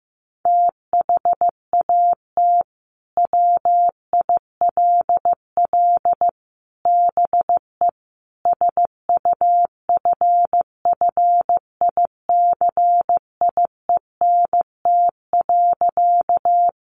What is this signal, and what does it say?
Morse code, heard as a keyed tone, about 15 wpm, 700 Hz. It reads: THAT WILL BE SUFFICIENT.
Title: Sentences from Top 1400 Words 15wpm